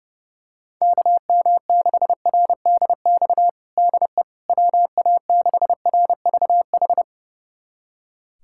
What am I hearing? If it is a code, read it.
Morse code, heard as a keyed tone, about 30 wpm, 700 Hz. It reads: KM6RDX DE WA6RVH
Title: USA Callsign DE Callsign - 6 Characters - 30wpm